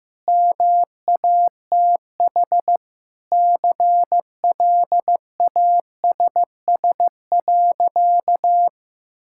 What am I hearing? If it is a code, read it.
Morse code, heard as a keyed tone, about 15 wpm, 700 Hz. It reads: MATH CLASS.
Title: Sentences from Top 1000 Words - Review Part 6 - 15wpm